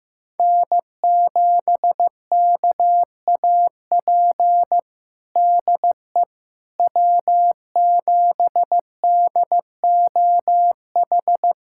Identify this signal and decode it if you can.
Morse code, heard as a keyed tone, about 15 wpm, 700 Hz. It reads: N7KAP DE W7DOH
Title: USA Callsign DE Callsign - 5 Characters - 15wpm